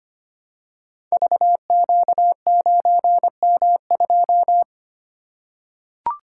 Morse code, heard as a keyed tone, about 25 wpm, 700 Hz. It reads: VQ9M2
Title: Code Groups - 5 Letter:Numbers - 25wpm